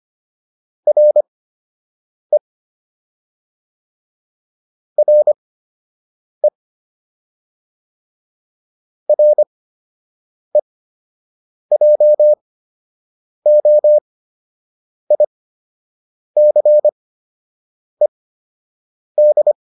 Morse code, heard as a keyed tone, about 25 wpm, spaced as for 7 wpm, 600 Hz. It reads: RE RE REJOICED
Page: fundamental-practice-files